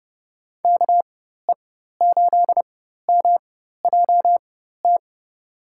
Morse code, heard as a keyed tone, about 31 wpm, spaced as for 14 wpm, 700 Hz. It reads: KE8MJT
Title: USA Call Signs - 6 Characters - 4x Character Spacing - 30wpm